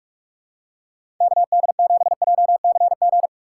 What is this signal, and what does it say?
Morse code, heard as a keyed tone, about 46 wpm, 700 Hz. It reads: KD7JCG
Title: USA Call Signs - 6 Characters - 4x Character Spacing - 45wpm